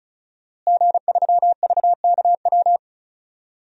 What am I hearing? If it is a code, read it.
Morse code, heard as a keyed tone, about 34 wpm, 700 Hz. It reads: G3VKW
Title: DXCC-100 Call Signs - 5 Characters - Repeated 2x - 35wpm